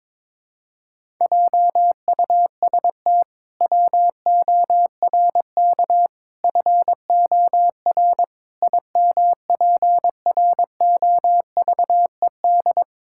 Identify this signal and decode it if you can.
Morse code, heard as a keyed tone, about 22 wpm, 700 Hz. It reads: JUST WORK FOR IMPROVED
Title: Pangram Challenge v2 - 22wpm